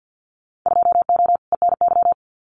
Morse code, heard as a keyed tone, about 50 wpm, 700 Hz. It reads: 2O ENY
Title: Winter Field Day Exchanges 50wpm